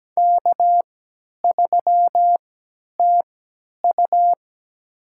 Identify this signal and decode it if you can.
Morse code, heard as a keyed tone, about 17 wpm, spaced as for 10 wpm, 700 Hz. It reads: K3TU
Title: USA Call Signs - 3x Character Spacing - 17wpm